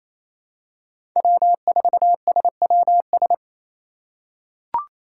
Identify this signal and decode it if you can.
Morse code, heard as a keyed tone, about 28 wpm, 700 Hz. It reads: W4SWS
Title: USA Call Signs - 4x Character Spacing - 28wpm